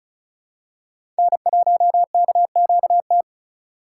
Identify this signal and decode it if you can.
Morse code, heard as a keyed tone, about 36 wpm, 690 Hz. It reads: N1KQT
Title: USA Call Signs - 5 Characters - 3x Character Spacing - 35wpm